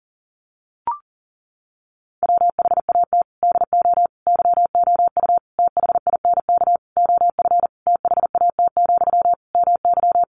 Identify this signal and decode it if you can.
Morse code, heard as a keyed tone, about 40 wpm, 700 Hz. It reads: WHAT DO YOU THINK OF THAT, MY
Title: Sentences from Top 1500 Words 40wpm